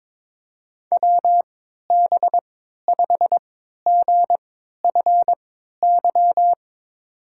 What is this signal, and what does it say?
Morse code, heard as a keyed tone, about 22 wpm, spaced as for 13 wpm, 700 Hz. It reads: WB5GFY
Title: USA Call Signs - 3x Character Spacing - 22wpm